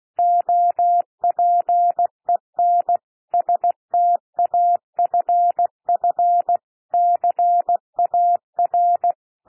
Morse code, heard as a keyed tone, about 16 wpm, 700 Hz. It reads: OPEN STAFF CAR
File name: cwmp3 / P1 SOUND PASSAGE 16 wpm